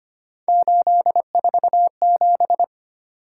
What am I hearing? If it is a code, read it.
Morse code, heard as a keyed tone, about 25 wpm, 700 Hz. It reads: 847
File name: Three Letter-Number Combinations - 25wpm